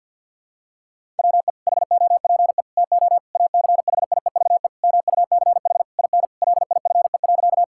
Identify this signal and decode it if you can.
Morse code, heard as a keyed tone, about 52 wpm, 700 Hz. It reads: WE HOPE TO ACHIEVE MUCH IN LIFE.
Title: Sentences from Top 1000 Words - Review Part 6 - 50wpm